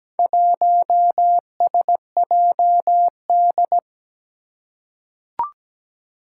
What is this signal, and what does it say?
Morse code, heard as a keyed tone, about 17 wpm, 700 Hz. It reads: 1SJD
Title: Four Challenging Letter-Number Combinations 17wpm